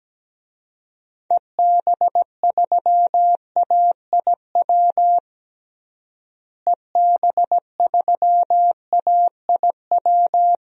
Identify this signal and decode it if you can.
Morse code, heard as a keyed tone, about 17 wpm, 700 Hz. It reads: EB3AIW EB3AIW
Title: DXCC-100 Call Signs - 6 Characters - Repeated 3x - 17wpm